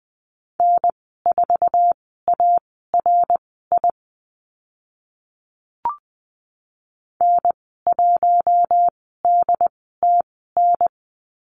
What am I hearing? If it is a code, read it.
Morse code, heard as a keyed tone, about 20 wpm, spaced as for 14 wpm, 700 Hz. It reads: N4ARI N1DTN
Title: 076 Lesson 13g - Letter L - Call Signs 20wpm